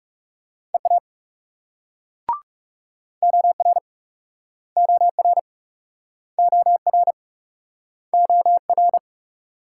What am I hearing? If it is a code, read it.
Morse code, heard as a keyed tone, about 38 wpm, 700 Hz. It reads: EA OR OR OR OR